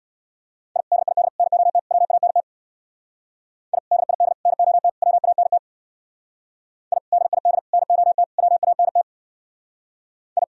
Morse code, heard as a keyed tone, about 62 wpm, 700 Hz. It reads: I DID NOT WANT I DID NOT WANT I DID NOT WANT I
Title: Speed-Racing - Sets of 4 Words 50wpm